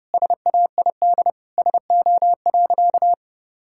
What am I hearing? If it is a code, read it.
Morse code, heard as a keyed tone, about 30 wpm, 700 Hz. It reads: SAID SO.
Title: Sentences from Top 1000 Words - Review Part 6 - 30wpm